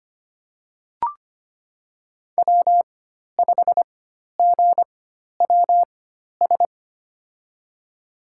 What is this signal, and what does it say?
Morse code, heard as a keyed tone, about 25 wpm, spaced as for 12 wpm, 700 Hz. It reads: W5GWS
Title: USA Call Signs - 5 Characters - 4x Character Spacing - 25wpm